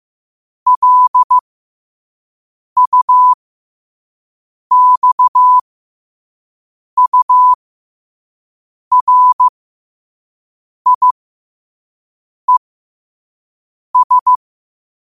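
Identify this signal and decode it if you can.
Morse code, heard as a keyed tone, about 15 wpm, spaced as for 5 wpm, 1000 Hz. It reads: LUXURIES
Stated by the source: Morse code practice at 5 wpm.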